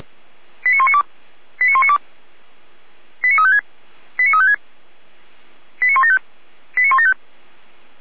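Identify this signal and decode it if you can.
Five-tone selective call (ZVEI), heard as quick runs of five tones, tones of about 70 ms. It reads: 89192 89192 89356 89356 89175 89175
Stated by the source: ZVEI five-tone call 89350 received on an SDR.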